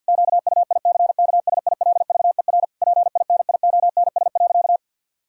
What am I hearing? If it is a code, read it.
Morse code, heard as a keyed tone, about 48 wpm, 700 Hz. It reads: QUICKSILVER PINIONS.
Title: Pangram Challenge v2 - 50wpm